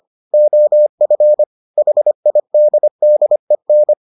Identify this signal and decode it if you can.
Morse code, heard as a keyed tone, about 25 wpm, 600 Hz. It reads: OF HIDDEN